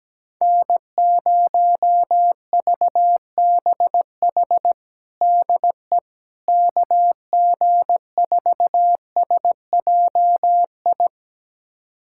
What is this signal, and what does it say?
Morse code, heard as a keyed tone, about 17 wpm, 700 Hz. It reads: N0VBH DE KG4SJI
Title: USA Callsign DE Callsign 17wpm